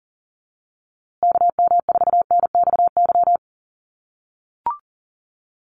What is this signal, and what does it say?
Morse code, heard as a keyed tone, about 40 wpm, 700 Hz. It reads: KM4NXY
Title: USA Call Signs Repeated 2x 40wpm